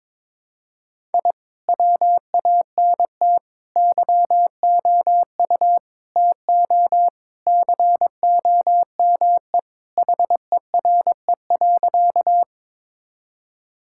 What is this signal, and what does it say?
Morse code, heard as a keyed tone, about 22 wpm, 700 Hz. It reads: I WANT YOU TO COME HERE.
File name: practice / 22wpm / Sentences with 2-Word Phrases - 22wpm